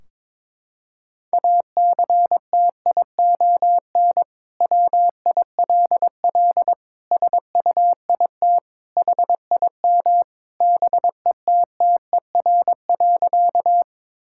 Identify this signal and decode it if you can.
Morse code, heard as a keyed tone, about 22 wpm, 700 Hz. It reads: ACTION WILL SUIT HIM BETTER.
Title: Sentences from Top 1600 Words 22wpm